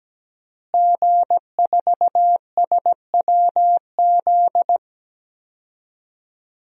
Morse code, heard as a keyed tone, about 17 wpm, 700 Hz. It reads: G4SWZ
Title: DXCC-100 Call Signs - 5 Characters - Repeated 3x - 17wpm